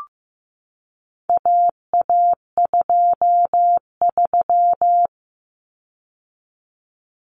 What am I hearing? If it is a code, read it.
Morse code, heard as a keyed tone, about 15 wpm, 700 Hz. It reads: AA23
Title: Common Cut Numbers - 4-Digit 15wpm